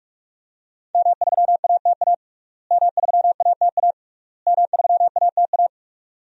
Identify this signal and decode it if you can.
Morse code, heard as a keyed tone, about 45 wpm, 700 Hz. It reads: M3ATA M3ATA M3ATA
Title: DXCC-100 Call Signs - 5 Characters - Repeated 3x - 45wpm